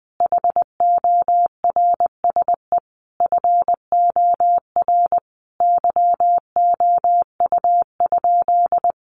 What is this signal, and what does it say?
Morse code, heard as a keyed tone, about 19 wpm, 700 Hz. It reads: HORSE FOR YOU?
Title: Sentences from Top 1400 Words 20wpm